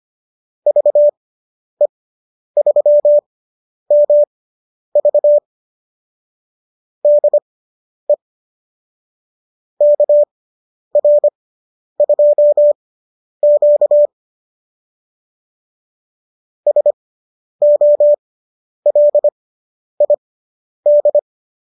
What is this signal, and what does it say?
Morse code, heard as a keyed tone, about 25 wpm, spaced as for 10 wpm, 590 Hz. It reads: VE3MV DE KR2Q SOLID